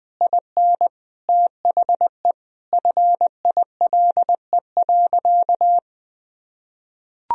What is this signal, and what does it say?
Morse code, heard as a keyed tone, about 20 wpm, 700 Hz. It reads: IN THE FILE.
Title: Sentences from Top 1000 Words - Review Part 4 - 20wpm